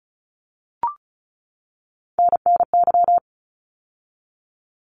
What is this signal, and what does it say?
Morse code, heard as a keyed tone, about 34 wpm, 700 Hz. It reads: NNY
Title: ARRL Sections 35wpm